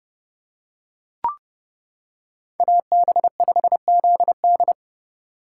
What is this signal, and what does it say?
Morse code, heard as a keyed tone, about 28 wpm, 700 Hz. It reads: AB5ZD